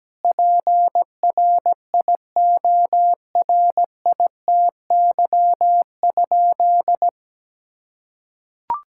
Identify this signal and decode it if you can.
Morse code, heard as a keyed tone, about 17 wpm, 700 Hz. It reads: PRIORITY?
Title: Sentences from Top 1600 Words 17wpm